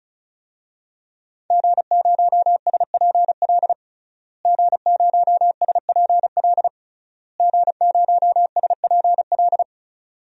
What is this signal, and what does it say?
Morse code, heard as a keyed tone, about 34 wpm, 700 Hz. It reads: G0SPL G0SPL G0SPL
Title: DXCC-100 Call Signs - 5 Characters - Repeated 3x - 35wpm